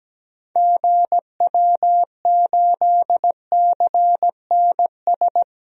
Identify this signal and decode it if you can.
Morse code, heard as a keyed tone, about 17 wpm, 700 Hz. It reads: GW8CNS
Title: DXCC-100 Call Signs - 6 Characters - Repeated 2x - 17wpm